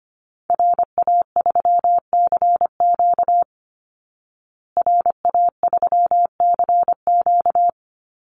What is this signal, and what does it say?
Morse code, heard as a keyed tone, about 25 wpm, 700 Hz. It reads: RA3CQ RA3CQ
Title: Call Signs Repeated 3x - 25wpm